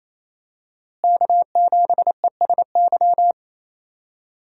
Morse code, heard as a keyed tone, about 28 wpm, 700 Hz. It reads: K7ESY
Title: USA Call Signs - 5 Characters - Repeated 2x - 28wpm